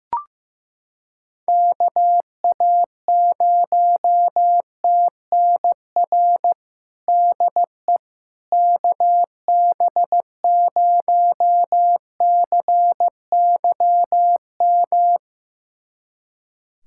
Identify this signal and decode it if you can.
Morse code, heard as a keyed tone, about 15 wpm, 700 Hz. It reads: KA0TNR DE KB0CYM
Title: USA Callsign DE Callsign 15wpm